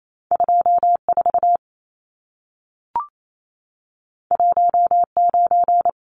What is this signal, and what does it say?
Morse code, heard as a keyed tone, about 28 wpm, 700 Hz. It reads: 24 19
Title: Numbers - 2-Digit 28wpm